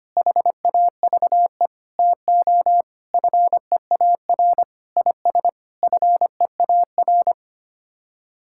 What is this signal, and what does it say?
Morse code, heard as a keyed tone, about 25 wpm, 700 Hz. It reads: HAVE TO FEAR IS FEAR
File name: Sets of 9 Words - 25wpm